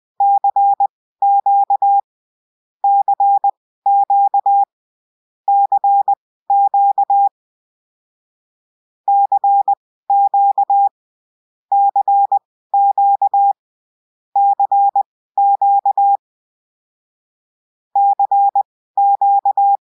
Morse code, heard as a keyed tone, about 20 wpm, spaced as for 14 wpm, 800 Hz. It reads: CQ CQ CQ CQ CQ CQ CQ